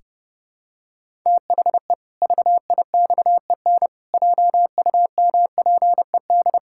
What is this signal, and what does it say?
Morse code, heard as a keyed tone, about 30 wpm, 700 Hz. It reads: THE VIXEN JUMPED
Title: Pangram Challenge v2 - 30wpm